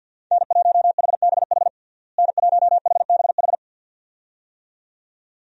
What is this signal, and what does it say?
Morse code, heard as a keyed tone, about 52 wpm, 700 Hz. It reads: N1HBH N1HBH
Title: USA Call Signs - 5 Characters - Repeated 2x - 50wpm